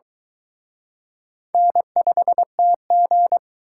Morse code, heard as a keyed tone, about 23 wpm, 700 Hz. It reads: N5TG